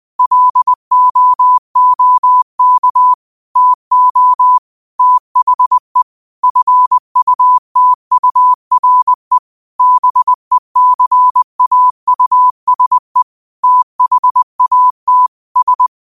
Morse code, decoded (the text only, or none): LOOK TO THE FUTURE BECAUSE THAT S